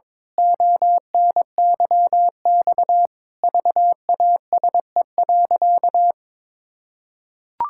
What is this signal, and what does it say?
Morse code, heard as a keyed tone, about 22 wpm, 700 Hz. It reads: ONYX VASE.
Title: Pangram Challenge v2 - 22wpm